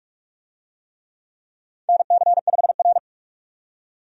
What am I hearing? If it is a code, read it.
Morse code, heard as a keyed tone, about 46 wpm, 700 Hz. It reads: NC5R
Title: USA Call Signs - 4 Characters - 45wpm